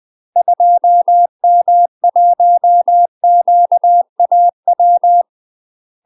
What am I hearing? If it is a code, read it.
Morse code, heard as a keyed tone, about 20 wpm, 700 Hz. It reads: 2M1QAW
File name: UK MASTER QSOs TRAINING / 20WPM / UK MASTER QSOs 0070